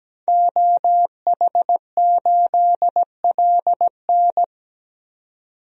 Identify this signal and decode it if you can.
Morse code, heard as a keyed tone, about 17 wpm, 700 Hz. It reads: OH8LN
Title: DXCC-100 Call Signs - 5 Characters - 17wpm